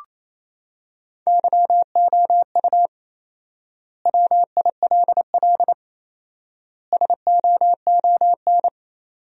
Morse code, heard as a keyed tone, about 28 wpm, 700 Hz. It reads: YOU WILL SOON
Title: 4x Word-Spacing - Sentences from Top 400 Words - 28wpm